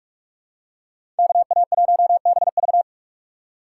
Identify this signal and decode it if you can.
Morse code, heard as a keyed tone, about 46 wpm, 700 Hz. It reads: KA1BV
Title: USA Call Signs - 5 Characters - 3x Character Spacing - 45wpm